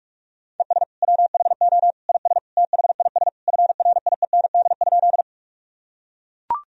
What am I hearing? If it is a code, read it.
Morse code, heard as a keyed tone, about 42 wpm, 700 Hz. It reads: ES WHO IS THIS FRIEND?
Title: Sentences from Top 300 Words and Abbreviations 45wpm